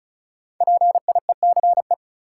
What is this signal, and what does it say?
Morse code, heard as a keyed tone, about 36 wpm, 700 Hz. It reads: PIECE
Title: Sets of 3 Words - New Word Drill - 35wpm